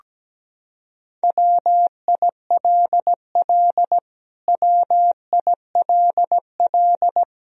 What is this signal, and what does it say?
Morse code, heard as a keyed tone, about 17 wpm, 700 Hz. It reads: WILL WILL
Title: Top 100 Words Repeated 17wpm